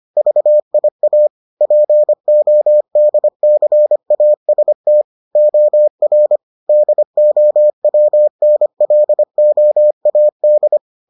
Morse code, heard as a keyed tone, about 25 wpm, 600 Hz. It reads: VIA PODCAST OR DOWNLOAD